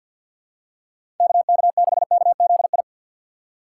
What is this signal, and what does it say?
Morse code, heard as a keyed tone, about 48 wpm, 700 Hz. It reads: KK6KZI
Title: USA Call Signs - 6 Characters - 6x Character Spacing - 50wpm